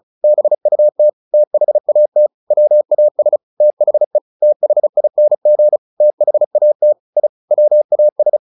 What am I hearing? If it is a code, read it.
Morse code, heard as a keyed tone, about 34 wpm, 600 Hz. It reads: BUT THAT WAS THE THING THAT I WAS
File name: the old man and the sea / the old man and the sea 1 35wpm